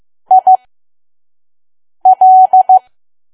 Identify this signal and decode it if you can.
Morse code, heard as a keyed tone, about 15 wpm, spaced as for 5 wpm, 750 Hz. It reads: IL
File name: Archive / 5 WPM files / PM